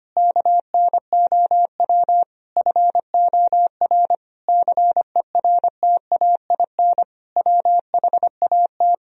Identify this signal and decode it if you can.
Morse code, heard as a keyed tone, about 25 wpm, 700 Hz. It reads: KNOW FOR CERTAIN WHAT
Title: Sentences from Top 1000 Words - Review Part 2 - 25wpm